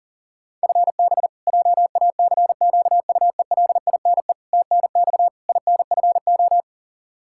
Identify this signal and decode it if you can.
Morse code, heard as a keyed tone, about 40 wpm, 700 Hz. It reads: FB JACQUELINE TNX INFO
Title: Practice QSO - Exchange 3 - More Info - 40wpm